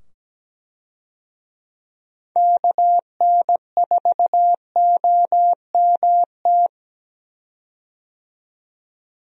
Morse code, heard as a keyed tone, about 17 wpm, 700 Hz. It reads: KN4OMT
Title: USA Call Signs - 6 Characters - 17wpm